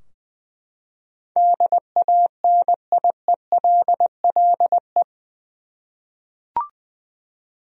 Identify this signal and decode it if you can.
Morse code, heard as a keyed tone, about 20 wpm, 700 Hz. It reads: DANIELLE
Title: USA Common Names - 20wpm